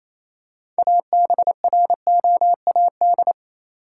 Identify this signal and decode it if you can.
Morse code, heard as a keyed tone, about 28 wpm, 700 Hz. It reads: ABROAD
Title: Six Letter Words - 28wpm